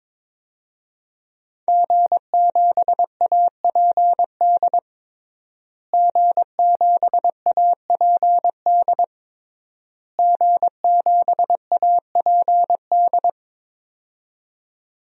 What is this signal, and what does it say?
Morse code, heard as a keyed tone, about 22 wpm, 700 Hz. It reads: G7APD G7APD G7APD
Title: DXCC-100 Call Signs - 5 Characters - Repeated 3x - 22wpm